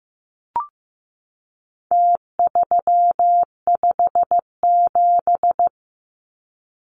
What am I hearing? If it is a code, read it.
Morse code, heard as a keyed tone, about 15 wpm, 700 Hz. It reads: T357
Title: Numbers - Contest Style - 4-Digit 15wpm